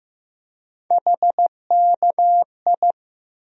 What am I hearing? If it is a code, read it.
Morse code, heard as a keyed tone, about 15 wpm, 700 Hz. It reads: HKI
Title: USA Call Sign Postfixes - 3 Letters 15wpm